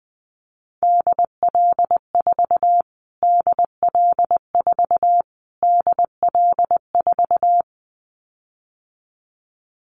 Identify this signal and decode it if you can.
Morse code, heard as a keyed tone, about 20 wpm, 700 Hz. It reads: DL4 DL4 DL4
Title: Callsign Prefixes for Top 25 DXCC - Repeated 3x - 20wpm